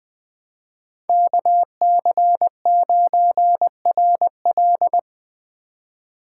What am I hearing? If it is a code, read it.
Morse code, heard as a keyed tone, about 20 wpm, 700 Hz. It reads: KC9RL